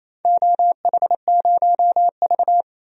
Morse code, heard as a keyed tone, about 28 wpm, 700 Hz. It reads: OH0V
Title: DXCC-100 Call Signs - 4 Characters - 28wpm